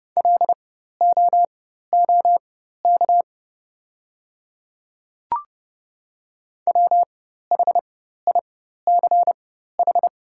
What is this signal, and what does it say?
Morse code, heard as a keyed tone, about 30 wpm, spaced as for 14 wpm, 690 Hz. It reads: LOOK WHICH
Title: 4x Character Spacing - Top 100 Words - 30wpm